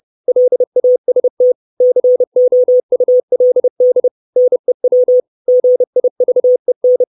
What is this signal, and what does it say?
Morse code, heard as a keyed tone, about 30 wpm, 500 Hz. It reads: LAST COULD NEW GIVEN
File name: 632 Words 30WPM